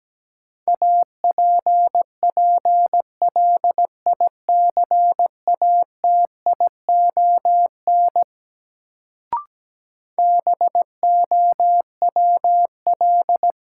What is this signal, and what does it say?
Morse code, heard as a keyed tone, about 17 wpm, 700 Hz. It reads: APPLICATION BOWL
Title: Sentences from Top 1200 Words - New Word Drill - 17wpm